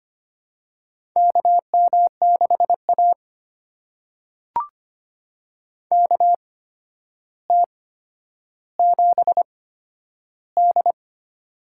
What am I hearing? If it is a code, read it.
Morse code, heard as a keyed tone, about 25 wpm, spaced as for 11 wpm, 700 Hz. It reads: KM6A KT7D
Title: USA Call Signs - 4 Characters - 8x Character Spacing - 25wpm